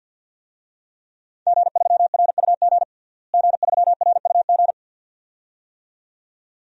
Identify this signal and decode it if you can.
Morse code, heard as a keyed tone, about 52 wpm, 700 Hz. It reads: G3RUG G3RUG
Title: DXCC-100 Call Signs - 5 Characters - Repeated 2x - 50wpm